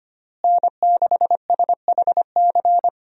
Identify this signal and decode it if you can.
Morse code, heard as a keyed tone, about 25 wpm, 700 Hz. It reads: N6SHC
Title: USA Call Signs - 5 Characters - Repeated 3x - 25wpm